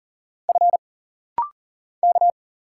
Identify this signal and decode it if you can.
Morse code, heard as a keyed tone, about 44 wpm, 700 Hz. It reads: F K
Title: Single Letter-Number Alt - Rapid-Fire 40wpm